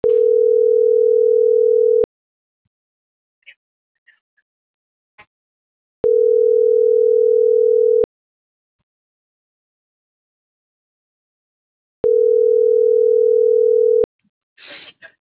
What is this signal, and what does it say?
Ringback tone (North America), 440 + 480 Hz.